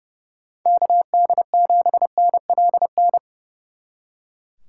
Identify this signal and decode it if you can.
Morse code, heard as a keyed tone, about 38 wpm, 700 Hz. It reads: KD7NLN
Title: Speed Racing - USA 2x3 Call Signs 30wpm